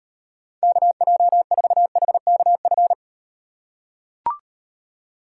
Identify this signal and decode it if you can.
Morse code, heard as a keyed tone, about 38 wpm, 700 Hz. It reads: KJ4HKF